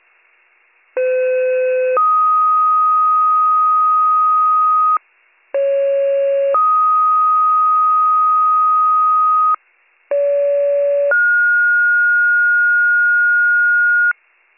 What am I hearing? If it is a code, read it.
Two-tone page, heard as a one-second tone, then a three-second tone, about 510 then 1250 Hz, about 550 then 1250 Hz, about 570 then 1430 Hz. It is heard from 1.0 to 14.1 s.